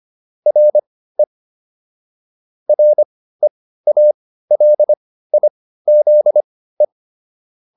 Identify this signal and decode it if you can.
Morse code, heard as a keyed tone, about 25 wpm, spaced as for 15 wpm, 600 Hz. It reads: RE REALIZE